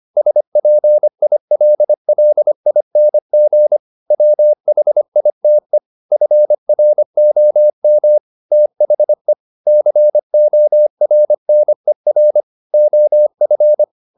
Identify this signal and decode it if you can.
Morse code, heard as a keyed tone, about 25 wpm, 600 Hz. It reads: SPILLING WHITE FROM THE CORNER OF